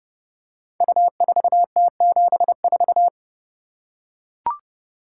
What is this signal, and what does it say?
Morse code, heard as a keyed tone, about 30 wpm, 700 Hz. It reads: U4T74
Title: Code Groups - 5 Letter:Numbers - 30wpm